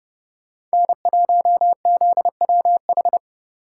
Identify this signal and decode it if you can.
Morse code, heard as a keyed tone, about 36 wpm, 700 Hz. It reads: N1ZWH
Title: Speed Racing - USA Call Signs 30wpm